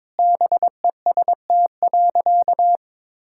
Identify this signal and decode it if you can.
Morse code, heard as a keyed tone, about 22 wpm, 700 Hz. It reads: BEST.
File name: practice / 22wpm / Sentences with 2-Word Phrases - 22wpm